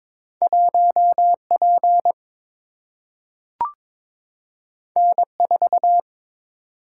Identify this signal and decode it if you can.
Morse code, heard as a keyed tone, about 22 wpm, 700 Hz. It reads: 1 P N 4
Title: Single-Letter Single-Number 22wpm